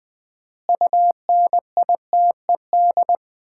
Morse code, heard as a keyed tone, about 21 wpm, 700 Hz. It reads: UNITED
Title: 4x Word-Spacing - Sets of 4 Words - 20wpm